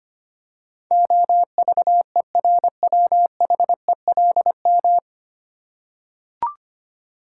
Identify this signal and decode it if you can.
Morse code, heard as a keyed tone, about 25 wpm, 690 Hz. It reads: OVERWHELM
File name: practice / 25wpm / 9 Letter Words - First 3 Letters Given - 25wpm